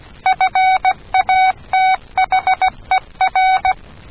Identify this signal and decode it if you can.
Morse code, heard as a keyed tone, about 16 wpm, 750 Hz. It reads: FATHER